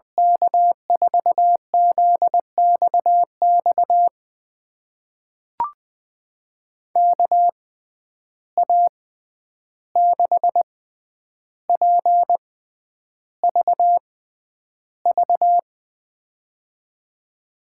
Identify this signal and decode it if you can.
Morse code, heard as a keyed tone, about 20 wpm, spaced as for 10 wpm, 700 Hz. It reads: K4ZXX KA6PVV
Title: USA Call Signs - 6 Characters - 6x Character Spacing - 20wpm